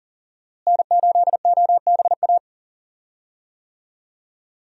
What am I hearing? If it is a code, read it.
Morse code, heard as a keyed tone, about 40 wpm, 700 Hz. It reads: N8OBA
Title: USA Call Signs - 5 Characters - Repeated 3x - 40wpm